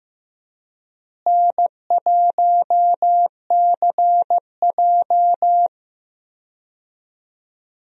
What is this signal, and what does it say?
Morse code, heard as a keyed tone, about 15 wpm, 700 Hz. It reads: N1CJ
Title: USA Call Signs - 4 Characters - Repeated 2x - 15wpm